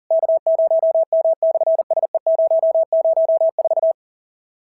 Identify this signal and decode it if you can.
Morse code, heard as a keyed tone, about 40 wpm, 690 Hz. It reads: K0M/SE004
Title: SOTA QSO Practice 40wpm